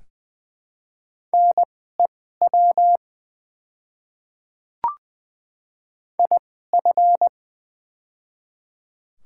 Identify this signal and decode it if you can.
Morse code, heard as a keyed tone, about 20 wpm, spaced as for 14 wpm, 690 Hz. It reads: NEW IF
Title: 2x Character Spacing - Top 500 Words - 20wpm